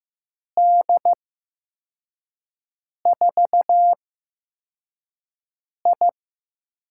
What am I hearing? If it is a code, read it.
Morse code, heard as a keyed tone, about 15 wpm, spaced as for 4 wpm, 700 Hz. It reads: D4I
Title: USA Call Signs - 6 Characters - 8x Character Spacing - 15wpm